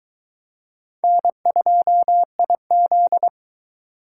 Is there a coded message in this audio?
Yes: Morse code from 1.0 to 3.3 s, heard as a keyed tone, about 22 wpm, 700 Hz. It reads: N2IZ